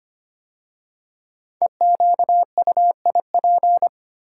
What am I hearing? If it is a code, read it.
Morse code, heard as a keyed tone, about 25 wpm, 700 Hz. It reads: EQUIP
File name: 5 Letter Words - First 3 Letters Given - 25wpm